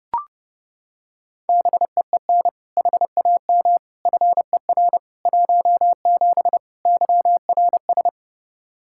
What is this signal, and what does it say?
Morse code, heard as a keyed tone, about 32 wpm, 700 Hz. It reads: BEEN HAM FER 17 YRS
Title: Practice QSO - Exchange 3 - More Info - 30wpm